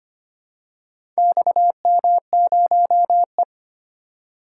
Morse code, heard as a keyed tone, about 25 wpm, 700 Hz. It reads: XM0E
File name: practice / 25wpm / Four Letter-Number Combinations - 25wpm